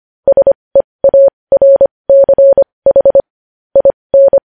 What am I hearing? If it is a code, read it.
Morse code, heard as a keyed tone, about 25 wpm, spaced as for 20 wpm, 550 Hz. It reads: SEARCH IN
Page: news-headlines-in-morse-code-at-20-wpm